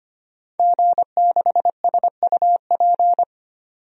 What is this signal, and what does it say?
Morse code, heard as a keyed tone, about 25 wpm, 700 Hz. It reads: G6SUP
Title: DXCC-100 Call Signs - 5 Characters - 25wpm